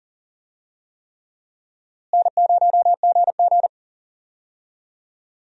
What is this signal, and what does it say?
Morse code, heard as a keyed tone, about 40 wpm, 700 Hz. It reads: N0GG
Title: USA Call Signs - 4 Characters - Repeated 3x - 40wpm